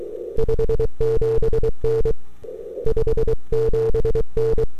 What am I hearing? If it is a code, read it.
Morse code, heard as a keyed tone, about 22 wpm, 440 Hz. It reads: 57N 57N
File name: n6tr / 12Oct2009-160